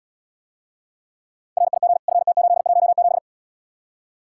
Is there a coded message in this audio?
Yes: Morse code from 1.6 to 3.2 s, heard as a keyed tone, about 75 wpm, 700 Hz. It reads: RIG FT817